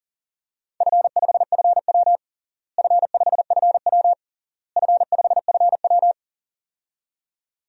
Morse code, heard as a keyed tone, about 40 wpm, 700 Hz. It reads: F5FW F5FW F5FW
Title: DXCC-100 Call Signs - 4 Characters - Repeated 3x - 40wpm